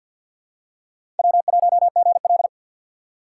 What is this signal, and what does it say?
Morse code, heard as a keyed tone, about 46 wpm, 700 Hz. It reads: W1GL
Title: USA Call Signs - 4 Characters - 3x Character Spacing - 50wpm